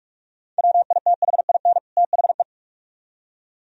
Morse code, heard as a keyed tone, about 59 wpm, spaced as for 52 wpm, 700 Hz. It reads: WITHIN THE
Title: Speed-Racing - Sets of 2 Words 45wpm